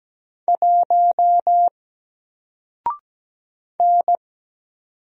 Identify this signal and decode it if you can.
Morse code, heard as a keyed tone, about 17 wpm, 700 Hz. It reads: 1 N
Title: Single Letter-Number Alt 17wpm